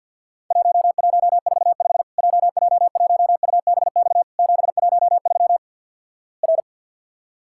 Morse code, heard as a keyed tone, about 53 wpm, 700 Hz. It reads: 1145 JJ1UBX 713 R
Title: Worked All Europe DX Contest - QTC Exchange with Repeats - 50wpm